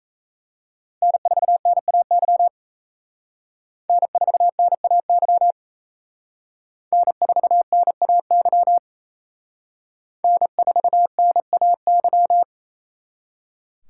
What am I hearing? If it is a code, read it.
Morse code, heard as a keyed tone, about 35 wpm, 700 Hz. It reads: N4NAY N4NAY N4NAY N4NAY